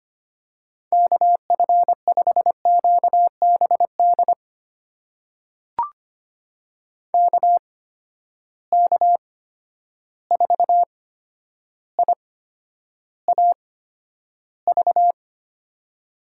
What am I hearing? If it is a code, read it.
Morse code, heard as a keyed tone, about 25 wpm, spaced as for 11 wpm, 700 Hz. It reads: KF5QBD KK4IAV
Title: USA Call Signs - 6 Characters - 8x Character Spacing - 25wpm